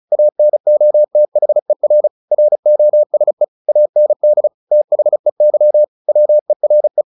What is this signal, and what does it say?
Morse code, heard as a keyed tone, about 34 wpm, 600 Hz. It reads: ANOTHER ROSE AND THEY WERE
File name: the old man and the sea / the old man and the sea 1 35wpm